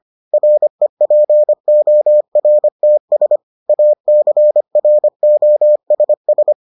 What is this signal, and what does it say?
Morse code, heard as a keyed tone, about 25 wpm, 600 Hz. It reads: REPORTS ACROSS